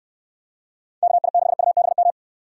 Morse code, heard as a keyed tone, about 69 wpm, 700 Hz. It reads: KI6UBM